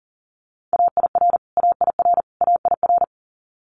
Morse code, heard as a keyed tone, about 40 wpm, 700 Hz. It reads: AIR AIR AIR
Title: Top 500 Words Repeated 40wpm